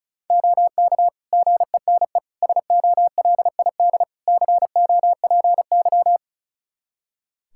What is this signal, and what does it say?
Morse code, heard as a keyed tone, about 36 wpm, 700 Hz. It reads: OK GENE SOLID COPY
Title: Practice QSO - Exchange 3 - More Info - 35wpm